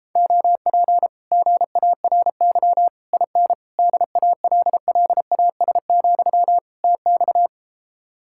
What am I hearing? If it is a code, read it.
Morse code, heard as a keyed tone, about 33 wpm, 700 Hz. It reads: OP GARY IN DALLAS, TX